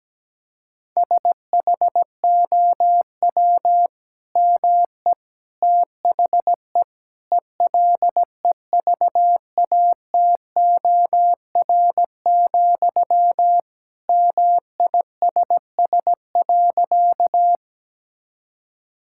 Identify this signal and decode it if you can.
Morse code, heard as a keyed tone, about 17 wpm, 700 Hz. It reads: SHOW ME THE ELEVATOR, MISS.
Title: Sentences from Top 1600 Words 17wpm